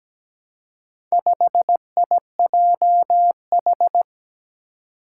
Morse code, heard as a keyed tone, about 17 wpm, 700 Hz. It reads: 5IJH
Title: Four Challenging Letter-Number Combinations 17wpm